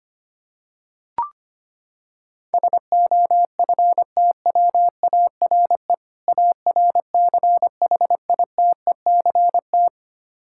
Morse code, heard as a keyed tone, about 25 wpm, 700 Hz. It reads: SOFTWARE ARCHITECT
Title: Practice QSO - Exchange 3 - More Info - 25wpm